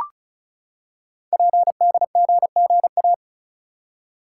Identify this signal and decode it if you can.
Morse code, heard as a keyed tone, about 36 wpm, 700 Hz. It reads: PDGGA
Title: Code Groups - 5 Letters - 35wpm